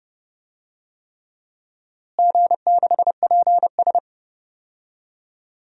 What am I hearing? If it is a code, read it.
Morse code, heard as a keyed tone, about 30 wpm, 700 Hz. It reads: G6PS
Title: DXCC-100 Call Signs - 4 Characters - Repeated 3x - 30wpm